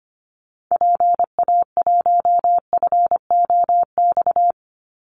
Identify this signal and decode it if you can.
Morse code, heard as a keyed tone, about 25 wpm, 700 Hz. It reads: PA1FOX
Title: Call Signs Repeated 2x - 25wpm